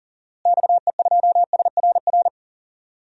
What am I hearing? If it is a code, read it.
Morse code, heard as a keyed tone, about 40 wpm, 700 Hz. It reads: XE2SRR
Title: DXCC-100 Call Signs - 6 Characters - Repeated 3x - 40wpm